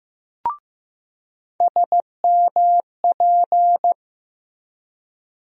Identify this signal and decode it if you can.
Morse code, heard as a keyed tone, about 15 wpm, 700 Hz. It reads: SMP
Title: USA Call Sign Postfixes - 3 Letters 15wpm